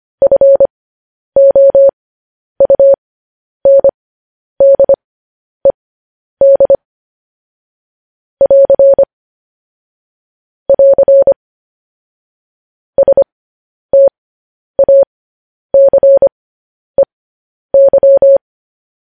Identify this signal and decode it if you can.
Morse code, heard as a keyed tone, about 25 wpm, spaced as for 10 wpm, 550 Hz. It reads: FOUNDED <AR> <AR> STACEY
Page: news-headlines-in-morse-code-at-10-wpm